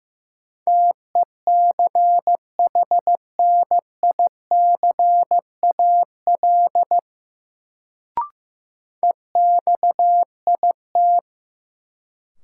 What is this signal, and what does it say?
Morse code, heard as a keyed tone, about 15 wpm, 700 Hz. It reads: TECHNICAL EXIT
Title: Sentences from Top 1100 Words - New Word Drill - 15wpm